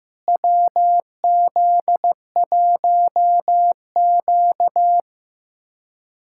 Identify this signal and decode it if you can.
Morse code, heard as a keyed tone, about 15 wpm, 700 Hz. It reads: WZ1Q
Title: USA Callsign DE Callsign - 4 Characters - 15wpm